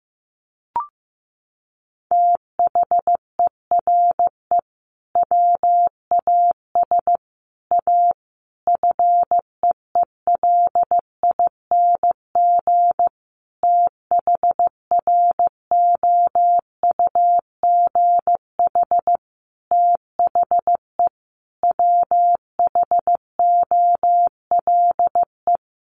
Morse code, heard as a keyed tone, about 15 wpm, 700 Hz. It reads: THERE WAS A FEELING THROUGH THE WHOLE
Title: Sentences from Top 1200 Words 15wpm